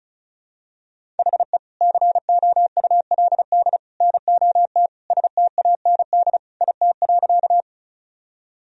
Morse code, heard as a keyed tone, about 35 wpm, 700 Hz. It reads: HE COULD NOT STAND IT.
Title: Sentences from Top 1000 Words - Review Part 1 - 35wpm